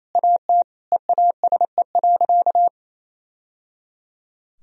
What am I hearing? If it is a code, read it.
Morse code, heard as a keyed tone, about 28 wpm, 700 Hz. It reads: AT EASE.
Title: Sentences from Top 1000 Words - Review Part 1 - 28wpm